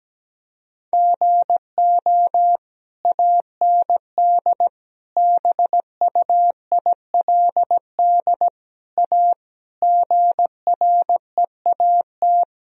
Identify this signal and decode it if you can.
Morse code, heard as a keyed tone, about 17 wpm, 700 Hz. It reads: GO AND BUILD A GREAT
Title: Sentences from Top 1000 Words - Review Part 1 - 17wpm